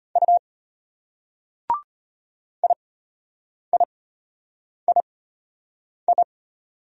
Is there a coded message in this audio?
Yes: Morse code from 0.2 to 6.2 s, heard as a keyed tone, about 32 wpm, spaced as for 8 wpm, 700 Hz. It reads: U IIII